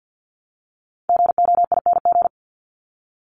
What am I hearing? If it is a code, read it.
Morse code, heard as a keyed tone, about 54 wpm, 700 Hz. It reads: GOING
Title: ING Suffix - Top 100 Words with Base Words 50wpm